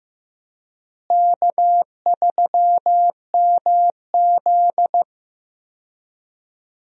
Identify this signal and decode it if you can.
Morse code, heard as a keyed tone, about 15 wpm, 700 Hz. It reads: K3MZ
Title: USA Call Signs - 4 Characters - Repeated 2x - 15wpm